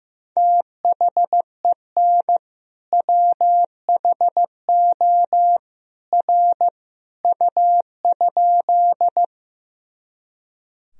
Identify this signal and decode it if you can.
Morse code, heard as a keyed tone, about 15 wpm, 700 Hz. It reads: THEN WHO R U?
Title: Sentences from Top 100 Words and Abbreviations 15wpm